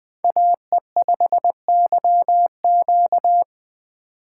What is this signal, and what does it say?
Morse code, heard as a keyed tone, about 20 wpm, 700 Hz. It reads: AE5YQ